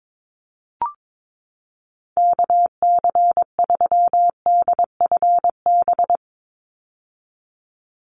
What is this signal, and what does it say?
Morse code, heard as a keyed tone, about 22 wpm, 700 Hz. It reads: KC3DFB
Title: USA Call Signs 22wpm